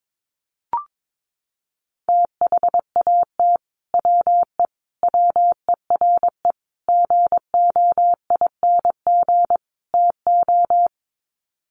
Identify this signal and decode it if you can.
Morse code, heard as a keyed tone, about 22 wpm, 700 Hz. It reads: THAT WE WERE GOING TO
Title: Sets of 5 Words Encore2 22wpm